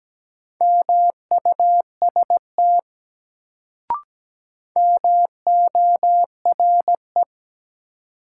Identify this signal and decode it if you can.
Morse code, heard as a keyed tone, about 17 wpm, 700 Hz. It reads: MUST MORE
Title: Sets of 2 Words - New Word Drill - 17wpm